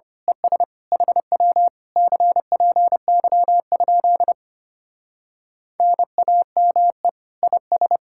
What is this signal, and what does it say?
Morse code, heard as a keyed tone, about 25 wpm, 700 Hz. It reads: ES HW CPY? NAME IS
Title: Speed-Racing Common QSO Phrases Encore - 25wpm